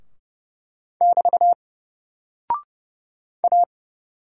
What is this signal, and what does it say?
Morse code, heard as a keyed tone, about 31 wpm, 700 Hz. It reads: <BT> A
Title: Single Character (Letter:Number:Punctuation:Prosign) 30wpm